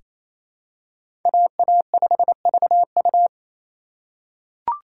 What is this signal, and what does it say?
Morse code, heard as a keyed tone, about 28 wpm, 700 Hz. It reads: AA5VU
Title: K1USN SST QSO Practice - 28wpm